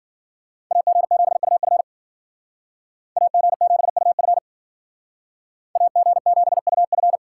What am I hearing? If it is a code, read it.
Morse code, heard as a keyed tone, about 52 wpm, 700 Hz. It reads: AG7UF AG7UF AG7UF